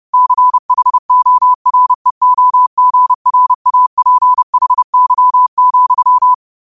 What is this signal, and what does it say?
Morse code, heard as a keyed tone, about 30 wpm, 1000 Hz. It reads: CHOREOGRAPHY,